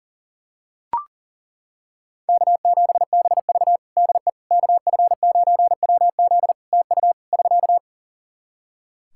Morse code, heard as a keyed tone, about 40 wpm, 700 Hz. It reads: K7BV DE KF9WZ TU <SK>
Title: K1USN SST QSO Practice 40wpm